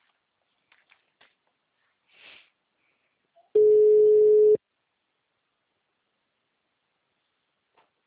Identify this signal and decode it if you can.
Ringback tone (Europe, 425 Hz).